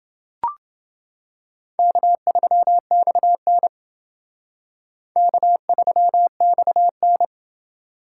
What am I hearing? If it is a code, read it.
Morse code, heard as a keyed tone, about 28 wpm, 700 Hz. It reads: K3XN K3XN